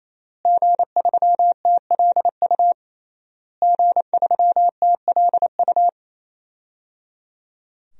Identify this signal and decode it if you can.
Morse code, heard as a keyed tone, about 28 wpm, 700 Hz. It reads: G3TLU G3TLU
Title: DXCC-100 Call Signs - 5 Characters - Repeated 3x - 28wpm